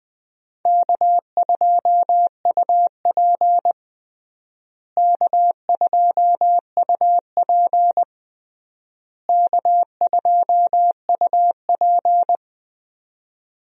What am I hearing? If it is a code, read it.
Morse code, heard as a keyed tone, about 20 wpm, 700 Hz. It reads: K2UP K2UP K2UP
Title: USA Call Signs - 4 Characters - Repeated 3x - 20wpm